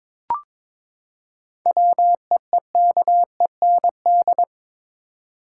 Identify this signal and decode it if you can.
Morse code, heard as a keyed tone, about 21 wpm, 700 Hz. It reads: WEEKEND
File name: practice / 22wpm / Build A Phrase - 5-Word Phrases - 22wpm